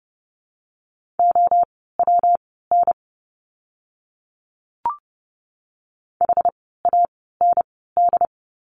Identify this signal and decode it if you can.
Morse code, heard as a keyed tone, about 30 wpm, spaced as for 17 wpm, 700 Hz. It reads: OWN HAND
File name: course / 30:10wpm / 115 Lesson 19f - Letter w - Words 30wpm